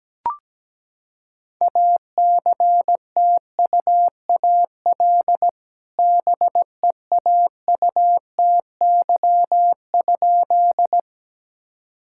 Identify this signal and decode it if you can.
Morse code, heard as a keyed tone, about 17 wpm, 700 Hz. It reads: ACTUAL BEAUTY?
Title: Sentences from Top 1000 Words - Review Part 5 - 17wpm